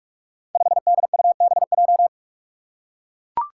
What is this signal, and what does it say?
Morse code, heard as a keyed tone, about 44 wpm, 700 Hz. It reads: 5DUBJ
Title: Five Challenging Letter-Number Combinations 45wpm